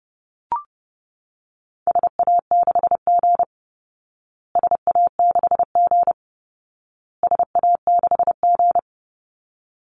Morse code, heard as a keyed tone, about 30 wpm, 700 Hz. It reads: SA6G SA6G SA6G
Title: Call Signs Repeated 2x - 30wpm